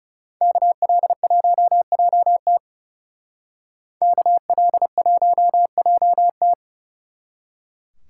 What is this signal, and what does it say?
Morse code, heard as a keyed tone, about 38 wpm, 700 Hz. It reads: KL1JT KL1JT